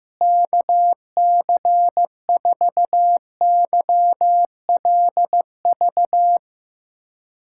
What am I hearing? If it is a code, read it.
Morse code, heard as a keyed tone, about 16 wpm, 700 Hz. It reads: KC4YLV
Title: K1USN SST QSO Practice - 15wpm